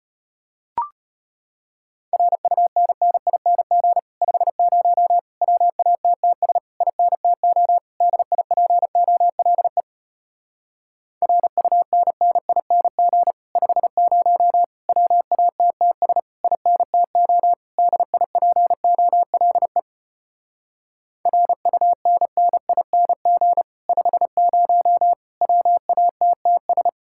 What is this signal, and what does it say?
Morse code, heard as a keyed tone, about 33 wpm, 700 Hz. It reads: RUNNING 50 WATTS INTO DIPOLE RUNNING 50 WATTS INTO DIPOLE RUNNING 50 WATTS